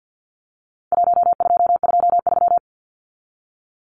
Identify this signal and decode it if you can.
Morse code, heard as a keyed tone, about 49 wpm, 700 Hz. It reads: 1223
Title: Numbers - Contest Style - 4-Digit 50wpm